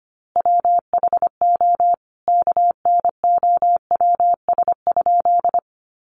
Morse code, heard as a keyed tone, about 25 wpm, 700 Hz. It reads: WHO KNOWS?
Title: A Sound Of Thunder 25wpm